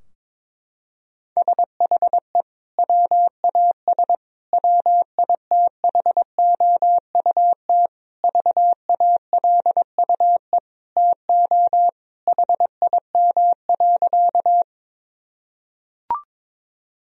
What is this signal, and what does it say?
Morse code, heard as a keyed tone, about 22 wpm, 700 Hz. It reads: SHE WAS WITHOUT VALUE TO HIM.
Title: Sentences from Top 1000 Words - Review Part 3 - 22wpm